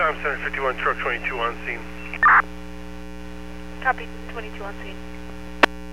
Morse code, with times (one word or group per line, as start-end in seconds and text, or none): none